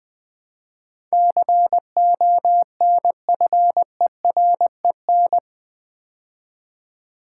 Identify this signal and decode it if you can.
Morse code, heard as a keyed tone, about 20 wpm, 700 Hz. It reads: CONFEREN
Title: Build A Word - 4 Letters At A Time - 10-Letter Words - 20wpm